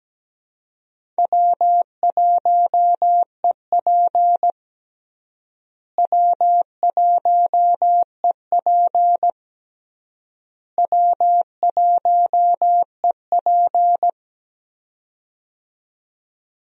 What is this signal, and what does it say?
Morse code, heard as a keyed tone, about 17 wpm, 700 Hz. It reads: W1EP W1EP W1EP
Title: USA Call Signs - 4 Characters - Repeated 3x - 17wpm